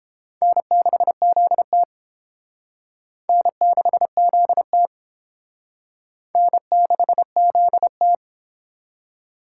Morse code, heard as a keyed tone, about 30 wpm, 700 Hz. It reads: N6ZT N6ZT N6ZT